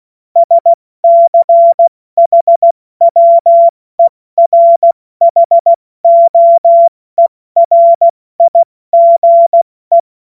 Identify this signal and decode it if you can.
Morse code, heard as a keyed tone, about 16 wpm, 680 Hz. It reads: SCHWERHOERIGE
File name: Morsekurs / Witze / Witz 28. Dezember 2021, 70 F80